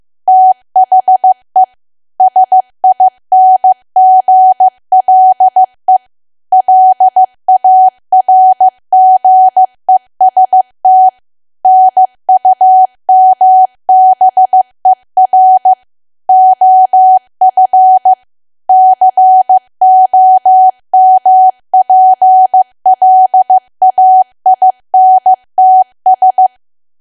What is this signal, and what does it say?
Morse code, heard as a keyed tone, about 15 wpm, 750 Hz. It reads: THE SINGLE LARGEST NUMBER OF COMPLAINTS
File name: cwa / W1AW-Level2 / 15WPM